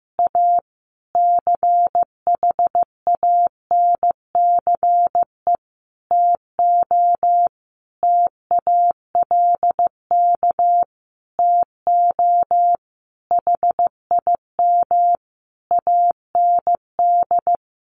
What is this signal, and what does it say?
Morse code, heard as a keyed tone, about 15 wpm, 700 Hz. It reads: A CHANCE TO TALK TO HIM AND
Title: Sentences from Top 1400 Words 15wpm